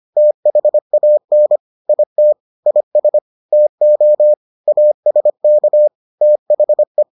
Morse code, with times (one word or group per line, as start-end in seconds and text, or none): 0.2-1.6 THAN
1.9-2.3 IT
2.7-3.2 IS
3.5-4.3 TO
4.7-5.9 ASK
6.2-7.0 THE